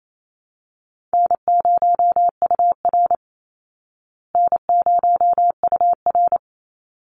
Morse code, heard as a keyed tone, about 28 wpm, 700 Hz. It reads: N0UR N0UR
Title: Easy Call Signs Repeated 2x - 28wpm